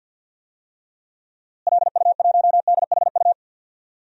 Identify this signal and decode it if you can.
Morse code, heard as a keyed tone, about 47 wpm, 700 Hz. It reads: LU1DHU
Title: DXCC-100 Call Signs - 6 Characters - Repeated 2x - 50wpm